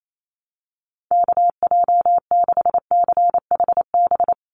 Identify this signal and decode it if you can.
Morse code, heard as a keyed tone, about 28 wpm, 700 Hz. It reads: KJ6CHB
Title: USA Call Signs 28wpm